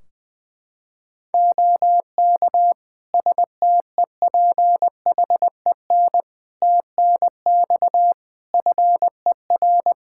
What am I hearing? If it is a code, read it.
Morse code, heard as a keyed tone, about 20 wpm, 700 Hz. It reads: OK STEPHEN TNX FER
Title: K1USN SST QSO Practice 20wpm